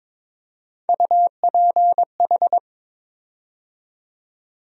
Morse code, heard as a keyed tone, about 22 wpm, 700 Hz. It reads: UPH
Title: USA Call Sign Postfixes - 3 Letters 22wpm